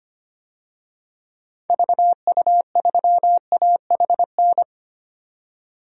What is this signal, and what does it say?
Morse code, heard as a keyed tone, about 25 wpm, 700 Hz. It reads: VU3AHN
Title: DXCC-100 Call Signs - 6 Characters - 25wpm